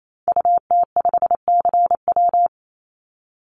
Morse code, heard as a keyed tone, about 28 wpm, 700 Hz. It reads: UT5CW
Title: Call Signs Repeated 2x - 28wpm